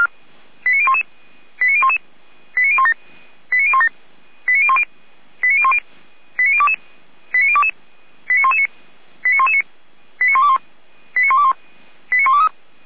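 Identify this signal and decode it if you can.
Five-tone selective call (ZVEI), heard as quick runs of five tones, tones of about 70 ms. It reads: 89011 89011 89017 89017 89010 89010 89022 89022 89119 89119 89121 89121 89123
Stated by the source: ZVEI five-tone selective call 89001 received on an SDR, served by feuerwehr-orlamuende (Orlamünde fire brigade).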